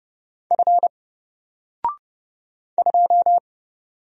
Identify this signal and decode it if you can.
Morse code, heard as a keyed tone, about 29 wpm, 700 Hz. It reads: F 2
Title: Single Character (Letter:Number:Punctuation:Prosign) 30wpm